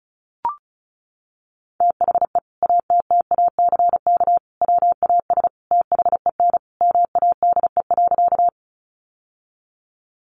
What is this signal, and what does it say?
Morse code, heard as a keyed tone, about 34 wpm, 700 Hz. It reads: THE ATTACK WAS THEN MADE.
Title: Sentences from Top 1500 Words 35wpm